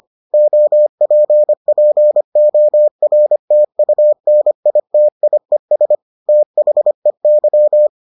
Morse code, heard as a keyed tone, about 25 wpm, 600 Hz. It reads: OPPORTUNITIES THEY